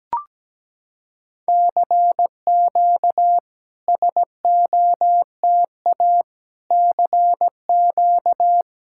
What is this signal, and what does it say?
Morse code, heard as a keyed tone, about 17 wpm, 700 Hz. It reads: CQ SOTA CQ
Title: SOTA QSO Practice 17wpm